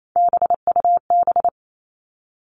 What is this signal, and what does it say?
Morse code, heard as a keyed tone, about 28 wpm, 700 Hz. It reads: BUB
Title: All Cut Numbers - 3-Digit 28wpm